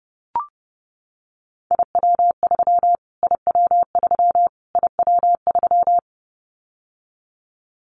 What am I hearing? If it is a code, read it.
Morse code, heard as a keyed tone, about 30 wpm, 700 Hz. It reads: IW3 IW3 IW3
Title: Callsign Prefixes for Top 25 DXCC - Repeated 3x - 30wpm